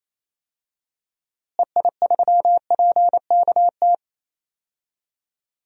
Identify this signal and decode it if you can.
Morse code, heard as a keyed tone, about 27 wpm, 700 Hz. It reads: EI3PKT
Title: DXCC-100 Call Signs - 6 Characters - Repeated 2x - 28wpm